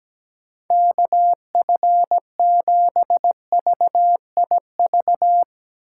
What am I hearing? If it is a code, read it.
Morse code, heard as a keyed tone, about 17 wpm, 700 Hz. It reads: KF7VIV
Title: USA Call Signs - 6 Characters - 6x Character Spacing - 17wpm